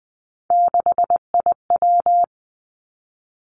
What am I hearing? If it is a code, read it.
Morse code, heard as a keyed tone, about 20 wpm, 700 Hz. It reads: 6IW
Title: Three Challenging Letter-Number Combinations 20wpm